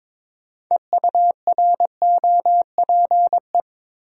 Morse code, heard as a keyed tone, about 22 wpm, 700 Hz. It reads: EUROPE
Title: Sets of 5 Words Encore - New Word Drill - 22wpm